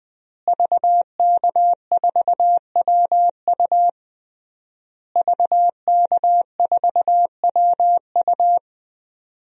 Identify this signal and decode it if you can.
Morse code, heard as a keyed tone, about 21 wpm, 700 Hz. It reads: VK4WU VK4WU
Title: DXCC-100 Call Signs - 5 Characters - Repeated 3x - 20wpm